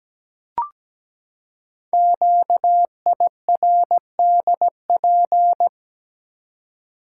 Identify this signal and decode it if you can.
Morse code, heard as a keyed tone, about 17 wpm, 700 Hz. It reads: QIRDP
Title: Code Groups - 5 Letter:Numbers - 17wpm